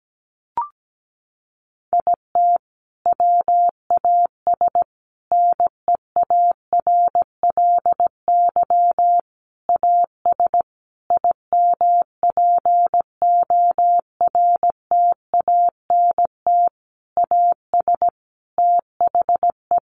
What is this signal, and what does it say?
Morse code, heard as a keyed tone, about 17 wpm, 700 Hz. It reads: IT WAS NEARLY AS IMPORTANT AS THE
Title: Sentences from Top 1400 Words 17wpm